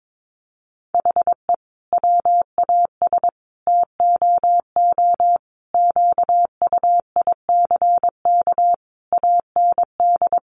Morse code, heard as a keyed tone, about 22 wpm, 700 Hz. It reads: HE WAS TOO QUICK AND
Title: Sentences from Top 600 Words 22wpm